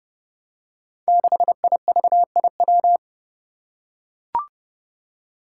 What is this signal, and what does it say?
Morse code, heard as a keyed tone, about 28 wpm, 700 Hz. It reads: 6IVIW
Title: Five Challenging Letter-Number Combinations 30wpm